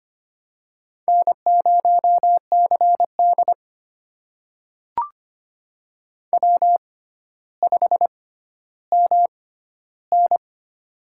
Morse code, heard as a keyed tone, about 25 wpm, spaced as for 13 wpm, 700 Hz. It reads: N0CD W5MN
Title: USA Call Signs - 6x Character Spacing - 25wpm